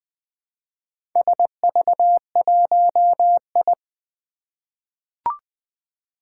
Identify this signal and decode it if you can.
Morse code, heard as a keyed tone, about 20 wpm, 700 Hz. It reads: SV1I
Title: Four Challenging Letter-Number Combinations 20wpm